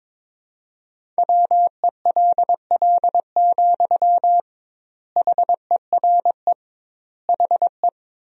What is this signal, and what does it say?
Morse code, heard as a keyed tone, about 22 wpm, 700 Hz. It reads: WELL, HERE HE
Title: 2x Word-Spacing - Sentences from Top 200 Words - 22wpm